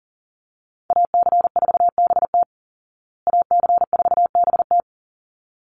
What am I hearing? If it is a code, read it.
Morse code, heard as a keyed tone, about 40 wpm, 700 Hz. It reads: AC4BT AC4BT
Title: Call Signs Repeated 3x - 40wpm